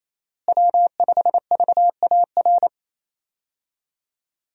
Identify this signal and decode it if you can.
Morse code, heard as a keyed tone, about 28 wpm, 700 Hz. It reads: W5VAR
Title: USA Call Signs - 5 Characters - Repeated 3x - 28wpm